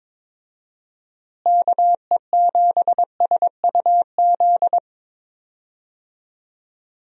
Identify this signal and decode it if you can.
Morse code, heard as a keyed tone, about 22 wpm, 700 Hz. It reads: KE7SUZ
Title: USA Call Signs - 6 Characters - Repeated 2x - 22wpm